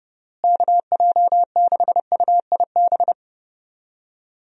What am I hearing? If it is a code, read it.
Morse code, heard as a keyed tone, about 30 wpm, 700 Hz. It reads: KJ6UIB
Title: USA Call Signs - 6 Characters - Repeated 3x - 30wpm